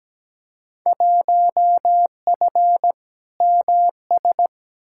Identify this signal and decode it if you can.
Morse code, heard as a keyed tone, about 17 wpm, 700 Hz. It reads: 1F MS
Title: Field Day Exchanges Encore - 17wpm